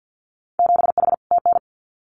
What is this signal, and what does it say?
Morse code, heard as a keyed tone, about 50 wpm, 700 Hz. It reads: 7H TN
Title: Winter Field Day Exchanges Encore 50wpm